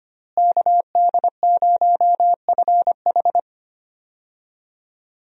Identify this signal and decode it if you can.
Morse code, heard as a keyed tone, about 25 wpm, 700 Hz. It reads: KD0FH
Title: USA Call Signs - 5 Characters - Repeated 2x - 25wpm